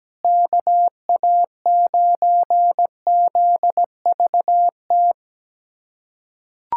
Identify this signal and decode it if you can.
Morse code, heard as a keyed tone, about 17 wpm, 700 Hz. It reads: KA9ZVT
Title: USA Callsign DE Callsign - 6 Characters - 17wpm